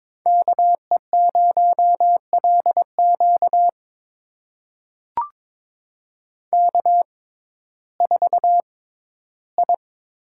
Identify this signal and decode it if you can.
Morse code, heard as a keyed tone, about 21 wpm, spaced as for 13 wpm, 700 Hz. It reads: KE0LQ K4I
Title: USA Call Signs - 5 Characters - 6x Character Spacing - 22wpm